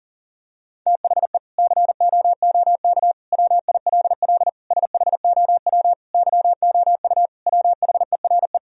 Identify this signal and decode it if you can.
Morse code, heard as a keyed tone, about 40 wpm, 700 Hz. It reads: THE COOK WILL SHOW YOU WHERE
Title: Sentences from Top 1000 Words - Review Part 4 - 40wpm